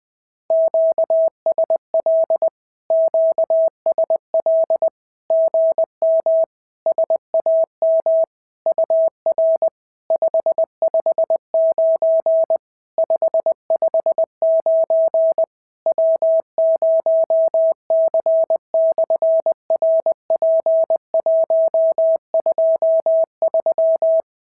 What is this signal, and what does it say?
Morse code, heard as a keyed tone, about 19 wpm, 650 Hz. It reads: QSL QSL GM SAM UR 559 559 W0C/RP123
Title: SOTA QSO Practice 20wpm